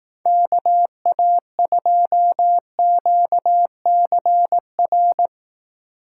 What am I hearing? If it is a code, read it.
Morse code, heard as a keyed tone, about 19 wpm, 700 Hz. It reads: KA2QCR
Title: Speed Racing - USA 2x3 Call Signs 15wpm